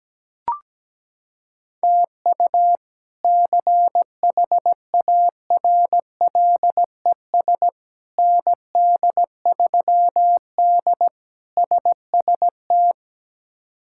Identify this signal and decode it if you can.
Morse code, heard as a keyed tone, about 17 wpm, 700 Hz. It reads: TU CHARLES ND3D SST
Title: K1USN SST QSO Practice - 17wpm